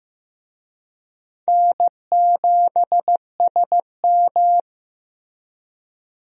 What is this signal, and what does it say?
Morse code, heard as a keyed tone, about 15 wpm, 700 Hz. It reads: N7SM
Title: USA Call Signs - 4 Characters - Repeated 2x - 15wpm